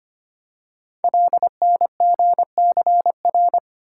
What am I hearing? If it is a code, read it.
Morse code, heard as a keyed tone, about 26 wpm, 700 Hz. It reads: LNGCR
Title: Code Groups - 5 Letters - 25wpm